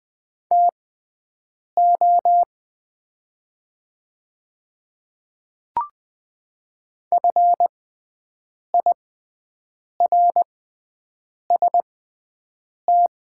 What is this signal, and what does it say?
Morse code, heard as a keyed tone, about 20 wpm, spaced as for 7 wpm, 700 Hz. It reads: TO FIRST
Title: 6x Character Spacing - Top 100 Words - 20wpm